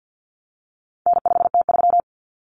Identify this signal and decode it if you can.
Morse code, heard as a keyed tone, about 50 wpm, 700 Hz. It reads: N5T3
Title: Common Cut Numbers - 4-Digit 50wpm